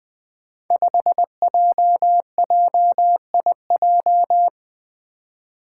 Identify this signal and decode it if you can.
Morse code, heard as a keyed tone, about 20 wpm, 700 Hz. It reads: 5JJIJ
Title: Five Challenging Letter-Number Combinations 20wpm